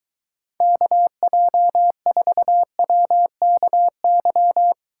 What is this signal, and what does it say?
Morse code, heard as a keyed tone, about 23 wpm, 700 Hz. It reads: KJ4WKY